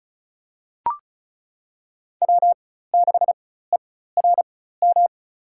Morse code, heard as a keyed tone, about 34 wpm, spaced as for 16 wpm, 700 Hz. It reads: W6ERM
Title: USA Call Signs - 5 Characters - 4x Character Spacing - 35wpm